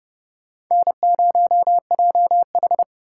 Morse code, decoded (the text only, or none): N0JH